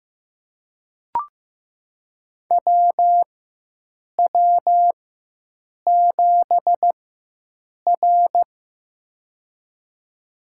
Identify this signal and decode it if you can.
Morse code, heard as a keyed tone, about 15 wpm, spaced as for 7 wpm, 700 Hz. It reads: WW7R
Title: USA Call Signs - 4 Characters - 4x Character Spacing - 15wpm